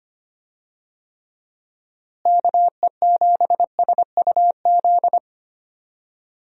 Morse code, heard as a keyed tone, about 25 wpm, 700 Hz. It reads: KE7SUZ
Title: USA Call Signs - 6 Characters - Repeated 2x - 25wpm